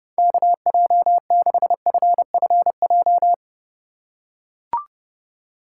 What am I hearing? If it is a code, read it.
Morse code, heard as a keyed tone, about 30 wpm, 700 Hz. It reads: KJ6FFJ